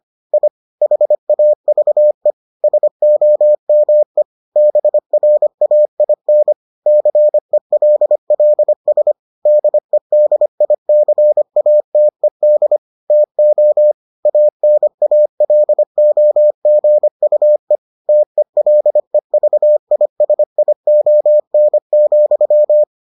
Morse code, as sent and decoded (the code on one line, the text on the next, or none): .. / .... .- ...- . / ... --- -- . / -... .-. .- .. -. / -.-. .-.. .-.. ... / -.. -.. .. -.-. .- - . -.. / - --- / .- -. .- .-.. --- --. ..- . / - . .-.. ...- .. ... .. --- -. --..--
I HAVE SOME BRAIN CELLS DEDICATED TO ANALOGUE TELEVISION,